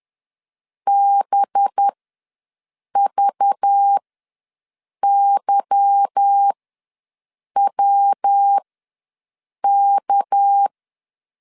Morse code, heard as a keyed tone, about 10 wpm, spaced as for 6 wpm, 800 Hz. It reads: BVYWK
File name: LEZIONI / 9 2867 ws 10wpm 4215